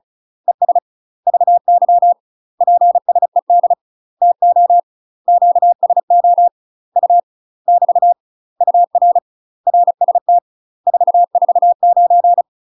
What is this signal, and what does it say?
Morse code, heard as a keyed tone, about 34 wpm, 700 Hz. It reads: ES VY PSED TO QSO U <BT> UR RST 449
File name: UK MASTER QSOs TRAINING / 35WPM / UK MASTER QSOs 0043